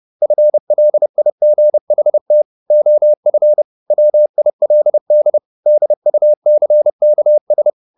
Morse code, heard as a keyed tone, about 30 wpm, 600 Hz. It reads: FLIGHT OF WILD DUCKS
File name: the old man and the sea / the old man and the sea 1 30wpm